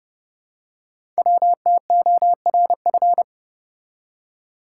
Morse code, heard as a keyed tone, about 30 wpm, 700 Hz. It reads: WTORF
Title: Code Groups - 5 Letter:Numbers - 30wpm